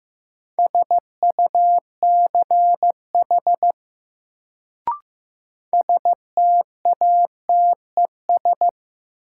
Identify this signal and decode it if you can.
Morse code, heard as a keyed tone, about 15 wpm, 700 Hz. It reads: SUCH STATES
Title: Sets of 2 Words - New Word Drill - 15wpm